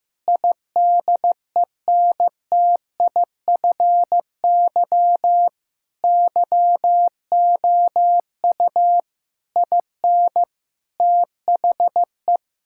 Morse code, heard as a keyed tone, about 15 wpm, 700 Hz. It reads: IDENTIFY YOU IN THE
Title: Sentences from Top 1000 Words - Review Part 6 - 15wpm